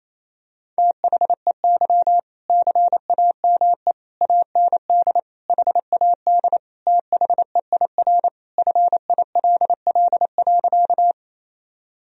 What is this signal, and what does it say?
Morse code, heard as a keyed tone, about 27 wpm, 700 Hz. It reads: THEY CAME AND HAD THEIR FILL.
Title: Sentences from Top 1000 Words - Review Part 3 - 28wpm